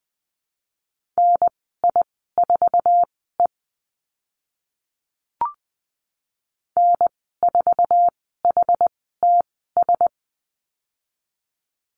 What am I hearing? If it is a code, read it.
Morse code, heard as a keyed tone, about 20 wpm, spaced as for 15 wpm, 700 Hz. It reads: NI4E N4HTS
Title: 062 Lesson 11g - Letter h - Call Signs 20wpm